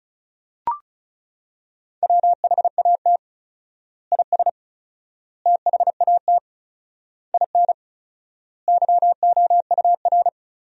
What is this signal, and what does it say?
Morse code, heard as a keyed tone, about 34 wpm, 700 Hz. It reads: WHAT IS THAT IN YOUR
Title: 4x Word-Spacing - Sentences from Top 200 Words - 35wpm